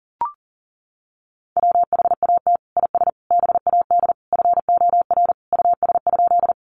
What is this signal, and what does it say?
Morse code, heard as a keyed tone, about 40 wpm, 700 Hz. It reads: WHAT IS BAD FOR US?
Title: Sentences from Top 700 Words 40wpm